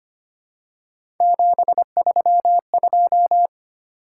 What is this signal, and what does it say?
Morse code, heard as a keyed tone, about 25 wpm, 700 Hz. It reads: 732
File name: Three Letter-Number Combinations - 25wpm